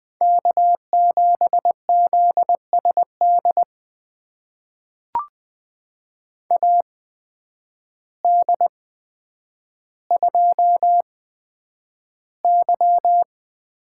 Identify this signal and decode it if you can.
Morse code, heard as a keyed tone, about 20 wpm, spaced as for 9 wpm, 700 Hz. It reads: K7ZSD AD2Y
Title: USA Call Signs - 8x Character Spacing - 20wpm